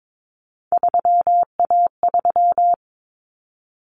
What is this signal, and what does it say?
Morse code, heard as a keyed tone, about 22 wpm, 700 Hz. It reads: 3A3
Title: Common Cut Numbers - 3-Digit 22wpm